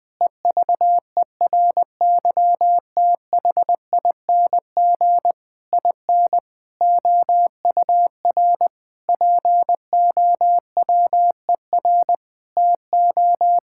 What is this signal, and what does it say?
Morse code, heard as a keyed tone, about 21 wpm, 700 Hz. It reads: EVERYTHING IN OUR POWER TO
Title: Sentences with 8-Word Phrases - Spoken Beginning - 20wpm